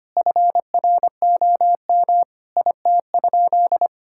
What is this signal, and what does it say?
Morse code, heard as a keyed tone, about 25 wpm, 700 Hz. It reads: FROM IT?
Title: Sentences from Top 1000 Words - Review Part 5 - 25wpm